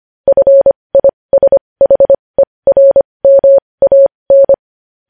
Morse code, heard as a keyed tone, about 25 wpm, spaced as for 20 wpm, 550 Hz. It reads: FISHERMAN
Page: News Headlines in Morse Code at 20 WPM